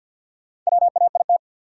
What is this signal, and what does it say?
Morse code, heard as a keyed tone, about 53 wpm, 700 Hz. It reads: WAIT
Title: Sentences from Top 500 Words - New Word Drill - 50wpm